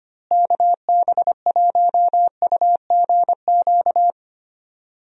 Morse code, heard as a keyed tone, about 25 wpm, 700 Hz. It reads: KB1UGQ
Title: USA Call Signs - 6 Characters - Repeated 3x - 25wpm